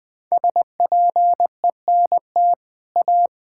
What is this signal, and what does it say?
Morse code, heard as a keyed tone, about 21 wpm, 700 Hz. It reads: SPENT A
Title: Build A Phrase - 5-Word Phrases - 20wpm